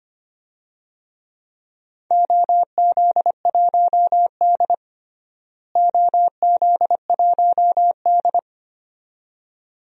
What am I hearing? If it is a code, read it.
Morse code, heard as a keyed tone, about 25 wpm, 700 Hz. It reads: OZ1D OZ1D
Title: DXCC-100 Call Signs - 4 Characters - Repeated 2x - 25wpm